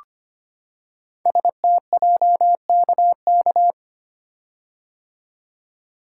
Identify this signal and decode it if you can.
Morse code, heard as a keyed tone, about 25 wpm, 700 Hz. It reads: STJKK
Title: Code Groups - 5 Letters - 25wpm